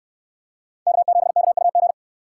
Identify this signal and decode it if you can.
Morse code, heard as a keyed tone, about 63 wpm, 700 Hz. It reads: K7KUG